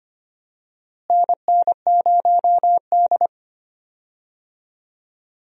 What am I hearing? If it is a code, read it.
Morse code, heard as a keyed tone, about 25 wpm, 700 Hz. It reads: NN0D
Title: USA Call Signs - 4 Characters - Repeated 2x - 25wpm